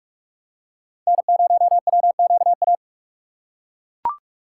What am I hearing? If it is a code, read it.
Morse code, heard as a keyed tone, about 41 wpm, 700 Hz. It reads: N0WQA